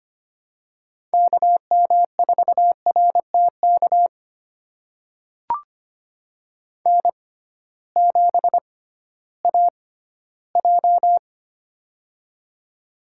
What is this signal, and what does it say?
Morse code, heard as a keyed tone, about 25 wpm, spaced as for 14 wpm, 700 Hz. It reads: KM4RTK N7AJ
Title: USA Call Signs - 6x Character Spacing - 25wpm